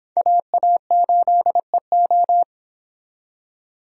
Morse code, heard as a keyed tone, about 24 wpm, 700 Hz. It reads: AA8EO